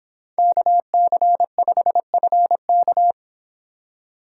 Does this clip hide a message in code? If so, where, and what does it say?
Yes: Morse code from 0.4 to 3.1 s, heard as a keyed tone, about 25 wpm, 700 Hz. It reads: KC5FK